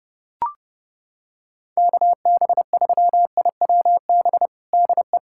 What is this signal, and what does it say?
Morse code, heard as a keyed tone, about 30 wpm, 700 Hz. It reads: KB3IWB DE
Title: USA Callsign DE Callsign - 6 Characters - 30wpm